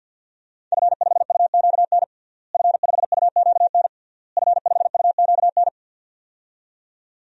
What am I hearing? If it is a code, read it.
Morse code, heard as a keyed tone, about 52 wpm, 700 Hz. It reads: F5UQN F5UQN F5UQN
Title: DXCC-100 Call Signs - 5 Characters - Repeated 3x - 50wpm